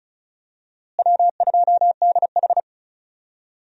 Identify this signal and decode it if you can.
Morse code, heard as a keyed tone, about 34 wpm, 700 Hz. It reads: W2DH
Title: USA Call Signs - 4 Characters - Repeated 3x - 35wpm